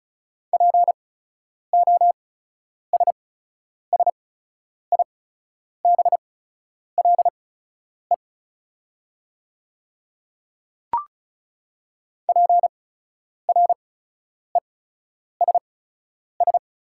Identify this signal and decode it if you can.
Morse code, heard as a keyed tone, about 36 wpm, spaced as for 10 wpm, 700 Hz. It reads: POSSIBLE PRESS
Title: 8x Character Spacing - Top 500 Words - 35wpm